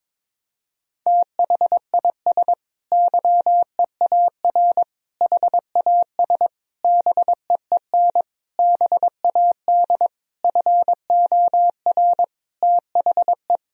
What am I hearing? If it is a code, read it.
Morse code, heard as a keyed tone, about 21 wpm, 700 Hz. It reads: THIS YEAR HAS BEEN BAD FOR THE
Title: Sentences from Top 1000 Words - Review Part 3 - 22wpm